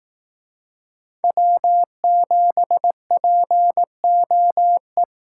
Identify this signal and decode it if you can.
Morse code, heard as a keyed tone, about 18 wpm, 700 Hz. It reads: W7POE
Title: Speed Racing - USA Call Signs 15wpm